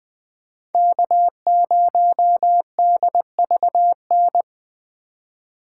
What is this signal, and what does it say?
Morse code, heard as a keyed tone, about 24 wpm, 700 Hz. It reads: K0DVN
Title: Speed Racing - USA Call Signs 20wpm